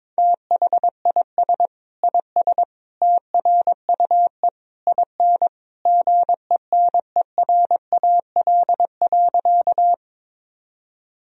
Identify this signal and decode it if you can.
Morse code, heard as a keyed tone, about 22 wpm, 700 Hz. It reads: THIS IS TRUE IN GENERAL.
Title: Sentences from Top 1000 Words - Review Part 3 - 22wpm